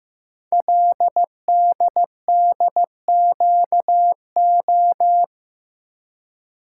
Morse code, heard as a keyed tone, about 15 wpm, 690 Hz. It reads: LDDQO
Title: Code Groups - 5 Letters - 15wpm